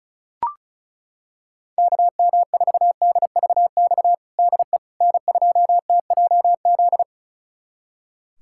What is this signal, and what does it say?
Morse code, heard as a keyed tone, about 36 wpm, 700 Hz. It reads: KM4DVX DE N2TJZ
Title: USA Callsign DE Callsign 35wpm